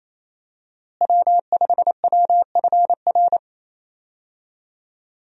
Morse code, heard as a keyed tone, about 28 wpm, 700 Hz. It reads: W5WFR
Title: USA Call Signs - 5 Characters - Repeated 3x - 28wpm